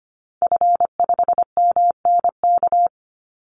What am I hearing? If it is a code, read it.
Morse code, heard as a keyed tone, about 25 wpm, 700 Hz. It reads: F5MNK
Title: Call Signs - 25wpm